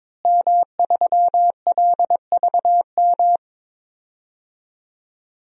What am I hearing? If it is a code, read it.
Morse code, heard as a keyed tone, about 22 wpm, 700 Hz. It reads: M3LVM
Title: DXCC-100 Call Signs - 5 Characters - Repeated 2x - 22wpm